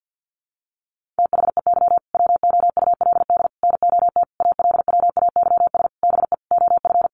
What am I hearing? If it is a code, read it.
Morse code, heard as a keyed tone, about 50 wpm, 700 Hz. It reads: THEY WOULD NOT ALWAYS BE OF
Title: Sentences from Top 1400 Words 50wpm